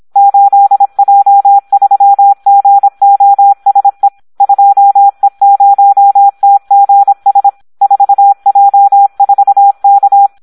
Morse code, heard as a keyed tone, about 26 wpm, 800 Hz. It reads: 8J3GOSE 2E0TGS 4J4K
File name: Audio practica CW / SD 1 26